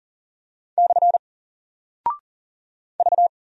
Morse code, heard as a keyed tone, about 40 wpm, 700 Hz. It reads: / V
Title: Single Character (Letter:Number:Punctuation:Prosign) Alt 40wpm